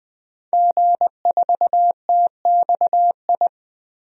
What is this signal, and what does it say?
Morse code, heard as a keyed tone, about 20 wpm, 700 Hz. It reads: G4TXI
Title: DXCC-100 Call Signs - 5 Characters - 20wpm